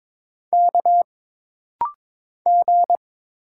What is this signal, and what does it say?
Morse code, heard as a keyed tone, about 22 wpm, 700 Hz. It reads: K G
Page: This rapid-fire timing is geared to push your ICR (Instant Character Recognition) skills.